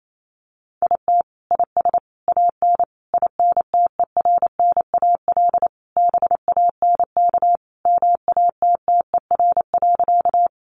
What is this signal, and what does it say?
Morse code, heard as a keyed tone, about 28 wpm, 700 Hz. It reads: IT IS AN INTERNAL BANK MATTER.
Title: Sentences from Top 1400 Words 28wpm